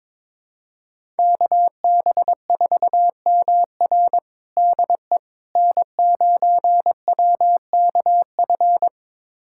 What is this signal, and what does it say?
Morse code, heard as a keyed tone, about 22 wpm, 700 Hz. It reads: KB4MR DE N9WKF
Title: USA Callsign DE Callsign - 5 Characters - 22wpm